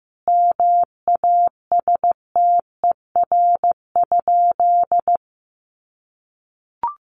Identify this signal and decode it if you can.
Morse code, heard as a keyed tone, about 14 wpm, 700 Hz. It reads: MASTER?
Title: Sentences from Top 1300 Words 15wpm